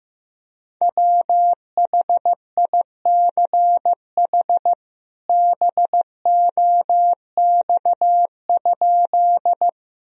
Morse code, heard as a keyed tone, about 16 wpm, 700 Hz. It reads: WHICH BOX?
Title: Sentences from Top 1000 Words - Review Part 2 - 15wpm